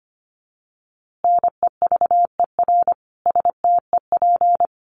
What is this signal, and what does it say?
Morse code, heard as a keyed tone, about 25 wpm, 700 Hz. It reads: NEVER STEP
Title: A Sound Of Thunder 25wpm